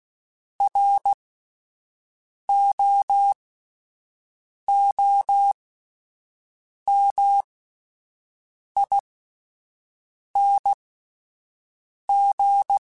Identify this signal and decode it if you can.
Morse code, heard as a keyed tone, about 16 wpm, spaced as for 5 wpm, 800 Hz. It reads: ROOMING